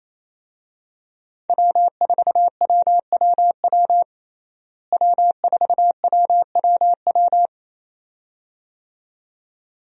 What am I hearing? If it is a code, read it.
Morse code, heard as a keyed tone, about 28 wpm, 700 Hz. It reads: W4WWW W4WWW
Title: USA Call Signs - 5 Characters - Repeated 2x - 28wpm